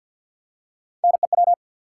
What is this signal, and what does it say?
Morse code, heard as a keyed tone, about 48 wpm, 700 Hz. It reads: NEW
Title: Three Letter Words - 50wpm